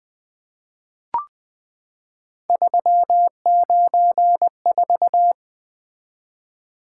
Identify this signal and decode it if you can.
Morse code, heard as a keyed tone, about 20 wpm, 700 Hz. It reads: 394
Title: Three Letter-Number Combinations - 20wpm